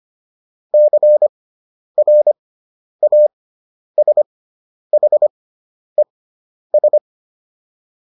Morse code, heard as a keyed tone, about 25 wpm, spaced as for 10 wpm, 600 Hz. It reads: CRASHES